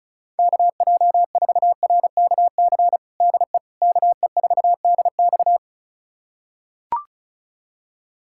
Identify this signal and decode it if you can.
Morse code, heard as a keyed tone, about 36 wpm, 700 Hz. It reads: KJ4RKC DE KE4DX
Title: USA Callsign DE Callsign 35wpm